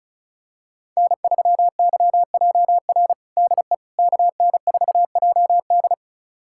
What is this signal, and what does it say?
Morse code, heard as a keyed tone, about 36 wpm, 700 Hz. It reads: N3YJR DE KN4JD
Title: USA Callsign DE Callsign - 5 Characters - 35wpm